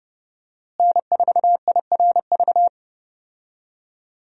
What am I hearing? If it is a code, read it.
Morse code, heard as a keyed tone, about 30 wpm, 700 Hz. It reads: N4IRV
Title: USA Call Signs - 5 Characters - Repeated 2x - 30wpm